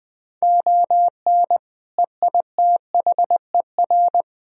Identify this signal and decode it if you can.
Morse code, heard as a keyed tone, about 22 wpm, 700 Hz. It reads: ON EITHER
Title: Speed-Racing - Sets of 5 Words 20wpm